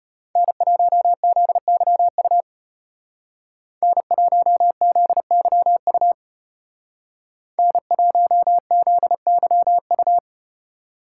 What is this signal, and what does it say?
Morse code, heard as a keyed tone, about 34 wpm, 700 Hz. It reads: N1ZYU N1ZYU N1ZYU